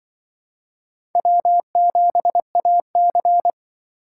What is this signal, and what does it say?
Morse code, heard as a keyed tone, about 23 wpm, 700 Hz. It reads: W7AC
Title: Speed Racing - USA 1x2 Call Signs 20wpm